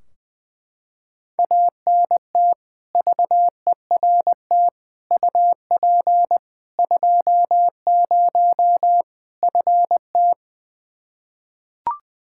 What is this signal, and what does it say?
Morse code, heard as a keyed tone, about 20 wpm, 700 Hz. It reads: ANT VERT UP 20 FT
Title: K1USN SST QSO Practice 20wpm